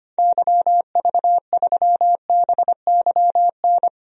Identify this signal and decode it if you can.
Morse code, heard as a keyed tone, about 25 wpm, 700 Hz. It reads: YV3BYN
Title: DXCC-100 Call Signs - 6 Characters - 25wpm